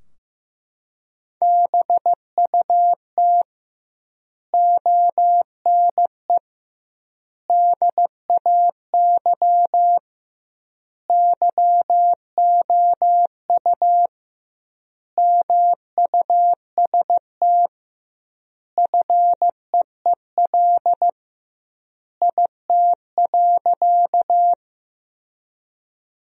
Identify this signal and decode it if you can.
Morse code, heard as a keyed tone, about 15 wpm, 700 Hz. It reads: BUT ONE DAY YOU MUST FEEL IT.
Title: 2x Word-Spacing - Sentences from Top 400 Words - 15wpm